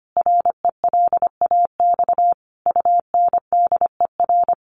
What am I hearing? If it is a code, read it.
Morse code, heard as a keyed tone, about 25 wpm, 700 Hz. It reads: RELAX UNDER
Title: Sentences from Top 1400 Words 25wpm